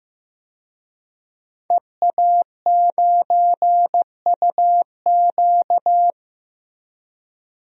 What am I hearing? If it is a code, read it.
Morse code, heard as a keyed tone, about 15 wpm, 700 Hz. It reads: EA9UQ
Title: DXCC-100 Call Signs - 5 Characters - 15wpm